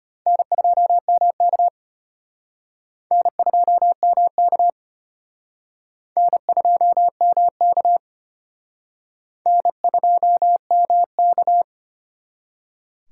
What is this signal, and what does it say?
Morse code, heard as a keyed tone, about 31 wpm, 700 Hz. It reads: N2MK N2MK N2MK N2MK